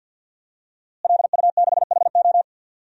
Morse code, heard as a keyed tone, about 45 wpm, 700 Hz. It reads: LU6HO
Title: DXCC-100 Call Signs - 5 Characters - Repeated 3x - 50wpm